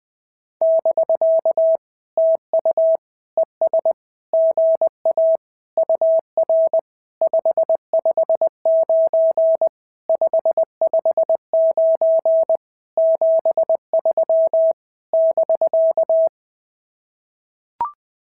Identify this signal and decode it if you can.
Morse code, heard as a keyed tone, about 20 wpm, 650 Hz. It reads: <BK> TU ES GA UR 559 559 73 <BK>
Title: SOTA QSO Practice 20wpm